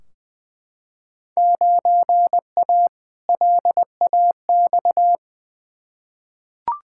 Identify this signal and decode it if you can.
Morse code, heard as a keyed tone, about 20 wpm, 700 Hz. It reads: 9A LAX
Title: Field Day Exchanges Encore - 20wpm